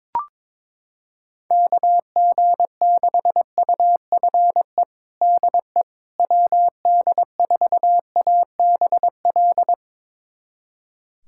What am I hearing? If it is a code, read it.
Morse code, heard as a keyed tone, about 22 wpm, 700 Hz. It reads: KG6UFE DE WD4ABL
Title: USA Callsign DE Callsign 22wpm